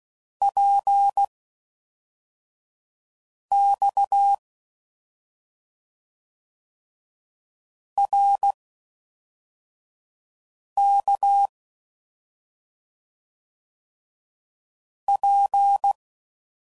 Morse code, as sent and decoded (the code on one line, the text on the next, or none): .--. / -..- / .-. / -.- / .--.
P X R K P